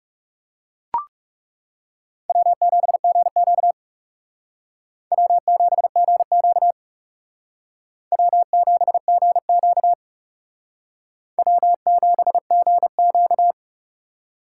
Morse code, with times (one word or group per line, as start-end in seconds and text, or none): 2.3-3.7 W7GQ
5.1-6.7 W7GQ
8.1-9.9 W7GQ
11.4-13.5 W7GQ